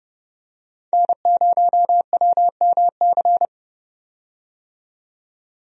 Morse code, heard as a keyed tone, about 29 wpm, 700 Hz. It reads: N0WMC
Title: USA Call Signs - 5 Characters - 30wpm